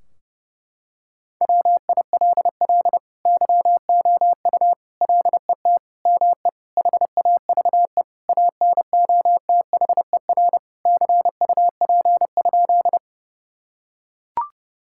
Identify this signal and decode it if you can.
Morse code, heard as a keyed tone, about 30 wpm, 700 Hz. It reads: WILL YOU LET ME HAVE ANOTHER CUP?
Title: Sentences from Top 1600 Words 30wpm